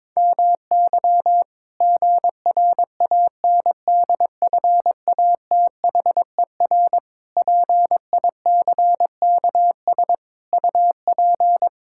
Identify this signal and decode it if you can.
Morse code, heard as a keyed tone, about 22 wpm, 700 Hz. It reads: MY GRANDFATHER PICKS UP
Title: Pangram Challenge v2 - 22wpm